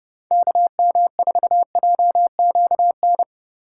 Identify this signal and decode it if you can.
Morse code, heard as a keyed tone, about 29 wpm, 700 Hz. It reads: KM4JQN